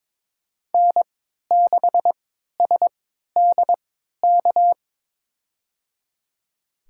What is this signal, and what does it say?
Morse code, heard as a keyed tone, about 22 wpm, spaced as for 13 wpm, 700 Hz. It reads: N6SDK
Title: USA Call Signs - 5 Characters - 3x Character Spacing - 22wpm